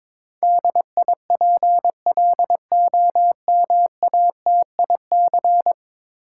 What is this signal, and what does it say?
Morse code, heard as a keyed tone, about 22 wpm, 700 Hz. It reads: DIPLOMATIC
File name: practice / 22wpm / 10 Letter Words - First 3 Letters Given - 22wpm